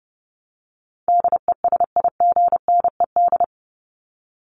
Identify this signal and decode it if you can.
Morse code, heard as a keyed tone, about 30 wpm, 700 Hz. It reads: DESIGNED
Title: ED Suffix - Top 100 Words with Base Words 30wpm